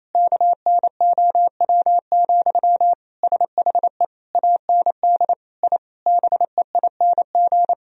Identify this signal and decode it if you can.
Morse code, heard as a keyed tone, about 28 wpm, 700 Hz. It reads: KNOW, SHE AND I BEING
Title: Sentences from Top 1000 Words - Review Part 3 - 28wpm